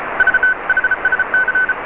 Morse code, heard as a keyed tone, about 34 wpm, 1500 Hz, with noise. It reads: V HSC